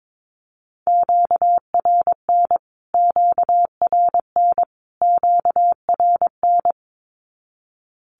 Morse code, heard as a keyed tone, about 22 wpm, 700 Hz. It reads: QRN QRN QRN
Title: Q-Codes Repeated 3x 22wpm